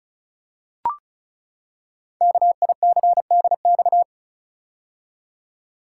Morse code, heard as a keyed tone, about 34 wpm, 700 Hz. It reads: KICDX
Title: Code Groups - 5 Letter:Numbers - 35wpm